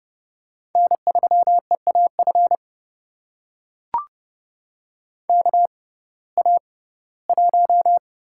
Morse code, heard as a keyed tone, about 30 wpm, spaced as for 18 wpm, 700 Hz. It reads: N3EAF KA1
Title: USA Call Signs - 6x Character Spacing - 30wpm